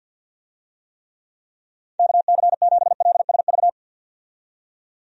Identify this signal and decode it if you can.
Morse code, heard as a keyed tone, about 45 wpm, 700 Hz. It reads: KC7LSV
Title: USA Call Signs - 6 Characters - 50wpm